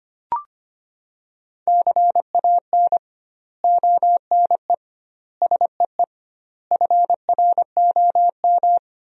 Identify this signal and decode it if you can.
Morse code, heard as a keyed tone, about 25 wpm, 700 Hz. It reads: CAN ONE SEE FROM
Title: 2x Word-Spacing - Sentences from Top 100 Words - 25wpm